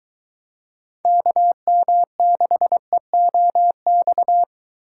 Morse code, heard as a keyed tone, about 23 wpm, 700 Hz. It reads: KM6EOX